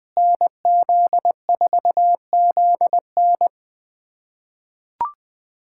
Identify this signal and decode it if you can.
Morse code, heard as a keyed tone, about 20 wpm, 700 Hz. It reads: NZ4ZN
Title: USA Callsign DE Callsign - 5 Characters - 20wpm